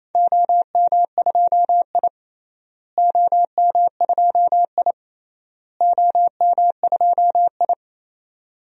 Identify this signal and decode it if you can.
Morse code, heard as a keyed tone, about 28 wpm, 700 Hz. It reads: OM2I OM2I OM2I
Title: DXCC-100 Call Signs - 4 Characters - Repeated 3x - 28wpm